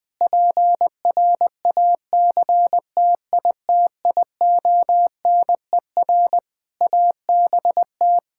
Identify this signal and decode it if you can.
Morse code, heard as a keyed tone, about 20 wpm, 700 Hz. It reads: PRACTITIONER ABT
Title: Practice QSO - Exchange 3 - More Info - 20wpm